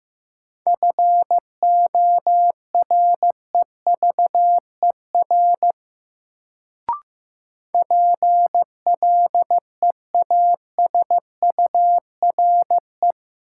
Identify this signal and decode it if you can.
Morse code, heard as a keyed tone, about 15 wpm, 700 Hz. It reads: FOREVER PLEASURE
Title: Sentences from Top 1600 Words - New Word Drill - 15wpm